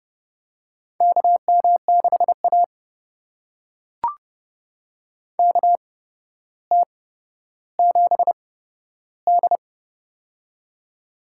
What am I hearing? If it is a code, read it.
Morse code, heard as a keyed tone, about 30 wpm, spaced as for 13 wpm, 700 Hz. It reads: KM6A KT7D
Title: USA Call Signs - 4 Characters - 8x Character Spacing - 30wpm